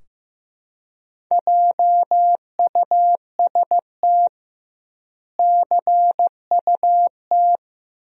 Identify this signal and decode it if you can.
Morse code, heard as a keyed tone, about 16 wpm, 700 Hz. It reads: JUST CUT
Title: 2x Word-Spacing - Sentences from Top 300 Words - 15wpm